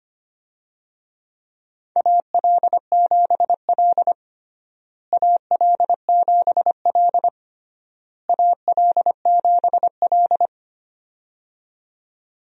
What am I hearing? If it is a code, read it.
Morse code, heard as a keyed tone, about 25 wpm, 700 Hz. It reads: AL7L AL7L AL7L
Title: USA Call Signs - 4 Characters - Repeated 3x - 25wpm